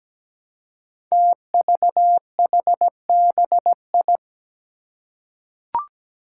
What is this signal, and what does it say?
Morse code, heard as a keyed tone, about 17 wpm, 700 Hz. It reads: TVHBI
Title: Five Challenging Letter-Number Combinations 17wpm